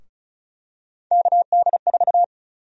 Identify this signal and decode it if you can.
Morse code, heard as a keyed tone, about 36 wpm, 700 Hz. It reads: KD4
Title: USA Call Sign Prefixes - 35wpm